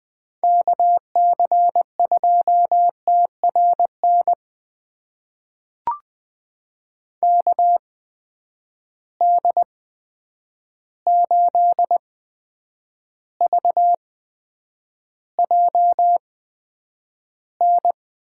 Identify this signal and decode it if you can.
Morse code, heard as a keyed tone, about 20 wpm, spaced as for 9 wpm, 700 Hz. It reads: KC2TRN KD8VJN
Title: USA Call Signs - 6 Characters - 8x Character Spacing - 20wpm